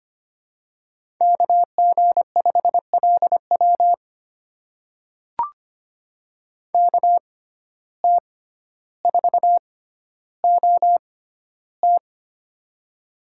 Morse code, heard as a keyed tone, about 25 wpm, spaced as for 13 wpm, 700 Hz. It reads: KG5LW KT4OT
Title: USA Call Signs - 5 Characters - 6x Character Spacing - 25wpm